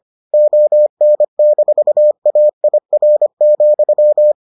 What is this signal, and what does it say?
Morse code, heard as a keyed tone, about 25 wpm, 600 Hz. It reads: ON-AIR,